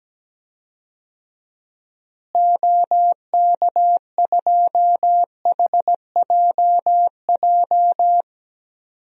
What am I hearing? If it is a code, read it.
Morse code, heard as a keyed tone, about 17 wpm, 700 Hz. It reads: OK2HJJ
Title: DXCC-100 Call Signs - 6 Characters - Repeated 2x - 17wpm